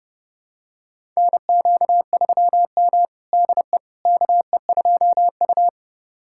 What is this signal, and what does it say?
Morse code, heard as a keyed tone, about 32 wpm, 700 Hz. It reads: NQ3M DE KE2U
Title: USA Callsign DE Callsign - 4 Characters - 30wpm